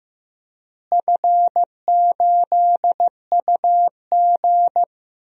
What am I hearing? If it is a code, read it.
Morse code, heard as a keyed tone, about 15 wpm, 700 Hz. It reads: F8UG
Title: DXCC-100 Call Signs - 4 Characters - Repeated 2x - 15wpm